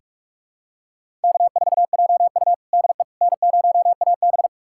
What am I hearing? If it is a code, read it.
Morse code, heard as a keyed tone, about 44 wpm, 700 Hz. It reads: K4JU DE N0AB
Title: USA Callsign DE Callsign - 4 Characters - 45wpm